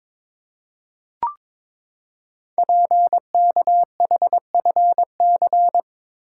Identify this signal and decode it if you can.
Morse code, heard as a keyed tone, about 22 wpm, 700 Hz. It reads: PKHFC
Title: Code Groups - 5 Letters - 22wpm